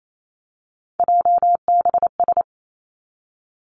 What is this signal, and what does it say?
Morse code, heard as a keyed tone, about 28 wpm, 700 Hz. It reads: JBS
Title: Three Challenging Letter-Number Combinations 28wpm